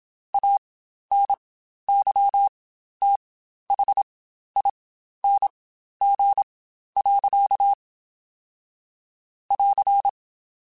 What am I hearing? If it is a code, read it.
Morse code, heard as a keyed tone, about 27 wpm, spaced as for 12 wpm, 800 Hz. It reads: ANYTHING. <AR>
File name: Quotes Spaced / Quotes Spaced-mp3s / StandFor sp @25